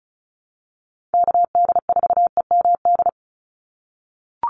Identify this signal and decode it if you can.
Morse code, heard as a keyed tone, about 34 wpm, 700 Hz. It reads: KD4EMD
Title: USA Call Signs 35wpm